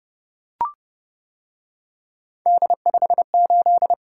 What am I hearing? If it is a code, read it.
Morse code, heard as a keyed tone, about 29 wpm, 700 Hz. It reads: D58